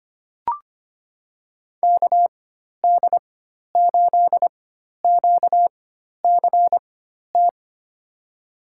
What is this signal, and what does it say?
Morse code, heard as a keyed tone, about 25 wpm, spaced as for 12 wpm, 700 Hz. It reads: KD8QCT
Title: USA Call Signs - 6 Characters - 4x Character Spacing - 25wpm